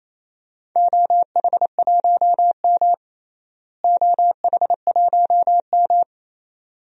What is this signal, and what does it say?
Morse code, heard as a keyed tone, about 28 wpm, 700 Hz. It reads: OH1M OH1M
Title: DXCC-100 Call Signs - 4 Characters - Repeated 2x - 28wpm